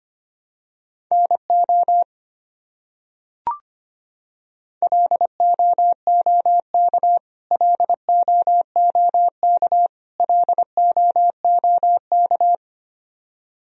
Morse code, heard as a keyed tone, about 25 wpm, 700 Hz. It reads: NO LOOK LOOK LOOK
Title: Top 100 Words Repeated 25wpm